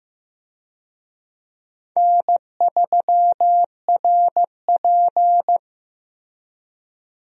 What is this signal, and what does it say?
Morse code, heard as a keyed tone, about 15 wpm, 700 Hz. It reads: N3RP
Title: USA Call Signs - 4 Characters - Repeated 2x - 15wpm